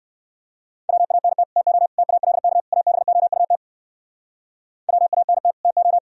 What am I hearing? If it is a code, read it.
Morse code, heard as a keyed tone, about 57 wpm, 700 Hz. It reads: WANT TO TALK ABOUT WANT TO